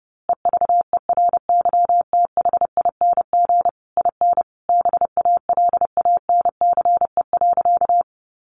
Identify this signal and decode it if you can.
Morse code, heard as a keyed tone, about 30 wpm, 700 Hz. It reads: EVERYTHING IN BALANCE.
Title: A Sound Of Thunder 30wpm